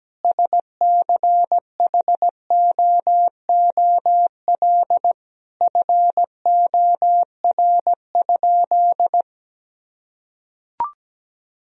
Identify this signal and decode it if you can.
Morse code, heard as a keyed tone, about 17 wpm, 700 Hz. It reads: SCHOOL FOR?
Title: Sentences from Top 1000 Words - Review Part 1 - 17wpm